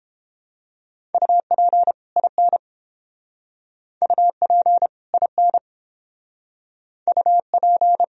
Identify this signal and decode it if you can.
Morse code, heard as a keyed tone, about 29 wpm, 700 Hz. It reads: UP IN UP IN UP